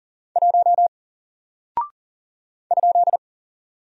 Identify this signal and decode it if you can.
Morse code, heard as a keyed tone, about 40 wpm, 700 Hz. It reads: 1 ?
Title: Single Character (Letter:Number:Punctuation:Prosign) 40wpm